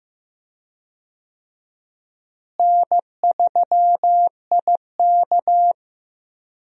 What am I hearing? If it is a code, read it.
Morse code, heard as a keyed tone, about 15 wpm, 700 Hz. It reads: N3IK
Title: USA Call Signs - 4 Characters - Repeated 3x - 15wpm